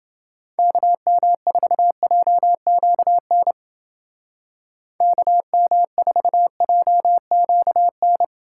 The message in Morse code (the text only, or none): KM4JQN KM4JQN